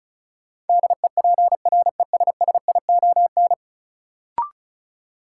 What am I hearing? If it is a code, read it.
Morse code, heard as a keyed tone, about 34 wpm, 700 Hz. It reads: DEPRESSION
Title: Sentences from Top 1300 Words - New Word Drill Alt - 35wpm